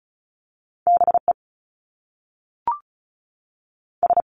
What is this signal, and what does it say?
Morse code, heard as a keyed tone, about 36 wpm, 700 Hz. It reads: B E S
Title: Top Two Letter Combinations in Words 35wpm